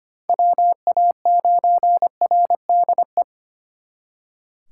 Morse code, heard as a keyed tone, about 32 wpm, 700 Hz. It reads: WA9RDE